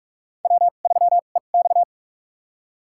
Morse code, heard as a keyed tone, about 47 wpm, spaced as for 33 wpm, 700 Hz. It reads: W3EX
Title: USA Call Signs - 4 Characters - 2x Character Spacing - 45wpm